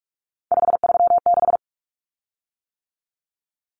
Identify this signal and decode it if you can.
Morse code, heard as a keyed tone, about 44 wpm, 700 Hz. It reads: 536
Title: Common Cut Numbers - 3-Digit 45wpm